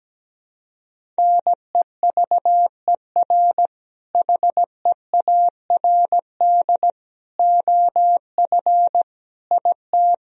Morse code, heard as a keyed tone, about 17 wpm, 700 Hz. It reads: NEVER HEARD OF IT
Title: Sentences from Top 400 Words and Abbreviations 17wpm